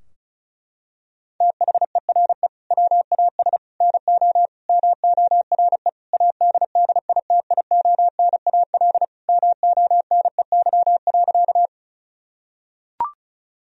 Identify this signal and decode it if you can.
Morse code, heard as a keyed tone, about 36 wpm, 700 Hz. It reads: THERE WAS NO MORE ADDITIONAL MONEY.
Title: Sentences from Top 1000 Words - Review Part 4 - 35wpm